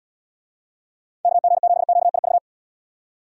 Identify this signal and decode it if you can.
Morse code, heard as a keyed tone, about 74 wpm, 700 Hz. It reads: KG7QIV